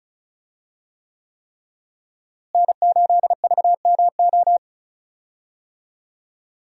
Morse code, heard as a keyed tone, about 36 wpm, 700 Hz. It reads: N8VMO
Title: USA Call Signs - 5 Characters - Repeated 3x - 35wpm